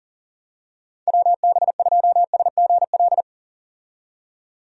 Morse code, heard as a keyed tone, about 40 wpm, 700 Hz. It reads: WB2SGL
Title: USA Call Signs - 6 Characters - 40wpm